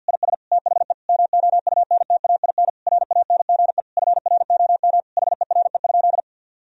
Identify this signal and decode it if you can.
Morse code, heard as a keyed tone, about 50 wpm, 700 Hz. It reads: IS THE MOUNTAIN RANGE FROM HERE?
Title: Sentences from Top 1000 Words - Review Part 4 - 50wpm